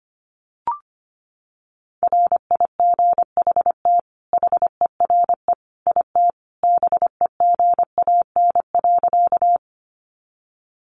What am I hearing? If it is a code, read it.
Morse code, heard as a keyed tone, about 25 wpm, 700 Hz. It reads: RIGHT HERE IT BEGAN.
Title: Sentences from Top 300 Words 25wpm